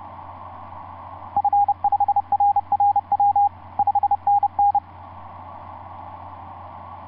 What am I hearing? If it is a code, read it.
Morse code, heard as a keyed tone, about 30 wpm, 810 Hz. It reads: F5RRW 5NN
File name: s79gm on4on etc 15m cw 30mar2010 0653